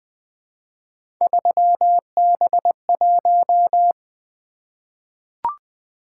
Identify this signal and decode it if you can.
Morse code, heard as a keyed tone, about 20 wpm, 700 Hz. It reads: 3B1
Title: Three Letter-Number Combinations - 20wpm